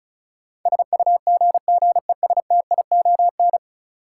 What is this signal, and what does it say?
Morse code, heard as a keyed tone, about 34 wpm, 700 Hz. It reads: SUGGESTION
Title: Sentences from Top 1600 Words - New Word Drill - 35wpm